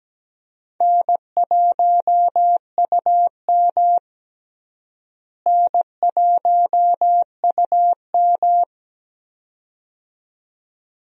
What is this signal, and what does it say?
Morse code, heard as a keyed tone, about 17 wpm, 700 Hz. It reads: N1UM N1UM
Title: USA Call Signs - 4 Characters - Repeated 2x - 17wpm